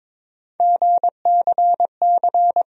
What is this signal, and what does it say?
Morse code, heard as a keyed tone, about 22 wpm, 700 Hz. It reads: GCC
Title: USA Call Sign Postfixes - 3 Letters 22wpm